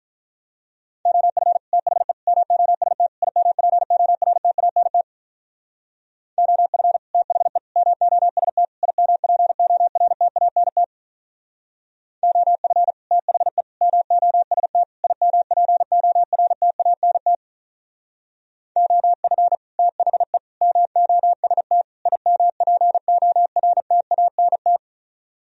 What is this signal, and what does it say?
Morse code, heard as a keyed tone, about 43 wpm, 700 Hz. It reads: OF THE MOST IMPORTANT OF THE MOST IMPORTANT OF THE MOST IMPORTANT OF THE MOST IMPORTANT
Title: Speed-Racing - Sets of 4 Words 35wpm